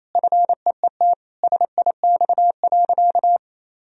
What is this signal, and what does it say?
Morse code, heard as a keyed tone, about 28 wpm, 700 Hz. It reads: FEET SIX.
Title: Sentences from Top 1000 Words - Review Part 2 - 28wpm